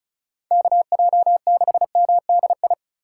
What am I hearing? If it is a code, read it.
Morse code, heard as a keyed tone, about 36 wpm, 700 Hz. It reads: KJ6MDI
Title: USA Call Signs - 6 Characters - 3x Character Spacing - 35wpm